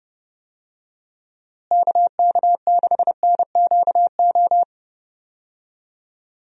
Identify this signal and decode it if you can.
Morse code, heard as a keyed tone, about 30 wpm, 700 Hz. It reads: KK6NQO
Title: USA Call Signs - 6 Characters - Repeated 2x - 30wpm